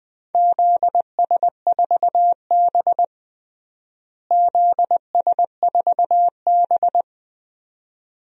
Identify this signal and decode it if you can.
Morse code, heard as a keyed tone, about 20 wpm, 700 Hz. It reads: ZS4B ZS4B
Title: DXCC-100 Call Signs - 4 Characters - Repeated 2x - 20wpm